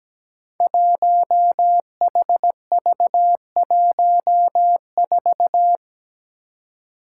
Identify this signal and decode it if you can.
Morse code, heard as a keyed tone, about 17 wpm, 700 Hz. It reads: 1HV14
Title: Five Challenging Letter-Number Combinations 17wpm